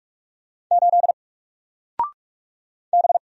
Morse code, heard as a keyed tone, about 44 wpm, 700 Hz. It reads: 8 B
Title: Single Letter-Number Alt 45wpm